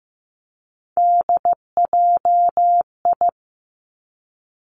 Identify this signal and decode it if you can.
Morse code, heard as a keyed tone, about 15 wpm, 700 Hz. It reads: DJI
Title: Three Challenging Letter-Number Combinations 15wpm